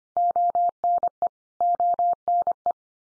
Morse code, heard as a keyed tone, about 25 wpm, 700 Hz. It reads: ONE ONE
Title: Top 100 Words 25wpm x3